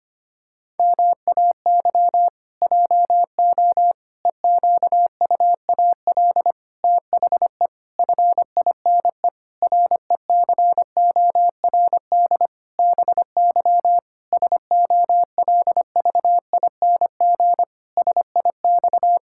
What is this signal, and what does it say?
Morse code, heard as a keyed tone, about 25 wpm, 700 Hz. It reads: MAY JO EQUAL THE FINE RECORD BY SOLVING SIX
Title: Pangram Challenge v2 - 25wpm